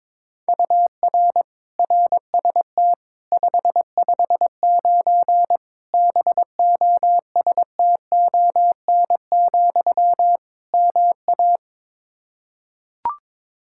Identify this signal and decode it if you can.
Morse code, heard as a keyed tone, about 22 wpm, 700 Hz. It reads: UR RST 559 BOSTON, MA
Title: Practice QSO - Exchange 1 - Beginning - 22wpm